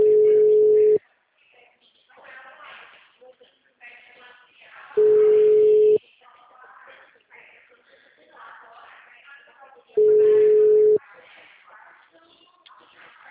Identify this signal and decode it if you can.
Ringback tone (Europe, 425 Hz), with speech.
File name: out- -2050- - - .58047